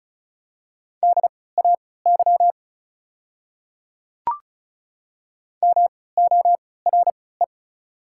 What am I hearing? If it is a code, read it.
Morse code, heard as a keyed tone, about 36 wpm, spaced as for 20 wpm, 690 Hz. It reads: DAY MORE
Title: 3x Character Spacing - Top 100 Words - 35wpm